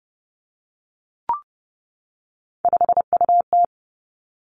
Morse code, heard as a keyed tone, about 30 wpm, 700 Hz. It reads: 5UT
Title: Three Challenging Letter-Number Combinations 30wpm